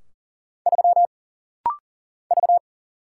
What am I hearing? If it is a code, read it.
Morse code, heard as a keyed tone, about 40 wpm, 700 Hz. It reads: 3 V
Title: Single Letter-Number - Rapid-Fire 40wpm